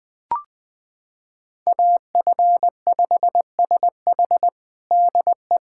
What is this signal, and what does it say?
Morse code, heard as a keyed tone, about 21 wpm, 700 Hz. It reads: AF5SH DE
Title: USA Callsign DE Callsign 20wpm